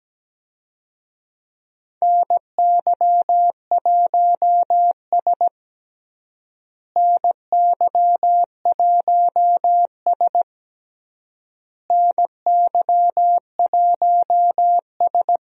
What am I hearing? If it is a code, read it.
Morse code, heard as a keyed tone, about 17 wpm, 700 Hz. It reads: NY1S NY1S NY1S
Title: USA Call Signs - 4 Characters - Repeated 3x - 17wpm